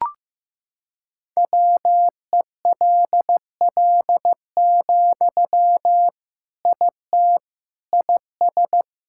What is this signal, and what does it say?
Morse code, heard as a keyed tone, about 15 wpm, 700 Hz. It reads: WELL, IT IS
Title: Sentences from Top 1600 Words 15wpm